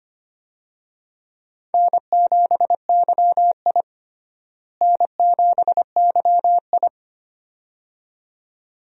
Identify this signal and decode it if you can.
Morse code, heard as a keyed tone, about 25 wpm, 700 Hz. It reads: N7YI N7YI
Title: USA Call Signs - 4 Characters - Repeated 2x - 25wpm